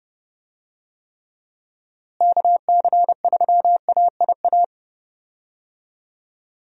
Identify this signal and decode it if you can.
Morse code, heard as a keyed tone, about 30 wpm, 700 Hz. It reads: KC3AIA
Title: USA Call Signs - 6 Characters - Repeated 3x - 30wpm